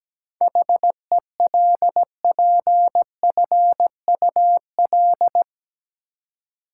Morse code, heard as a keyed tone, about 17 wpm, 700 Hz. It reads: HELPFUL
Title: Seven Letter Words - 17wpm